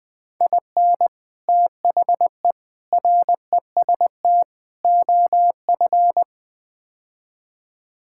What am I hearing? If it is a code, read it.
Morse code, heard as a keyed tone, about 21 wpm, 700 Hz. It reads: IN THE REST OF
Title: Build A Phrase - 5-Word Phrases - 20wpm